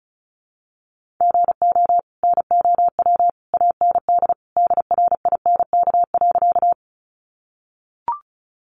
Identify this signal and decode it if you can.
Morse code, heard as a keyed tone, about 34 wpm, 700 Hz. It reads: GO NOW AND DRINK.
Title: Sentences from Top 1500 Words 35wpm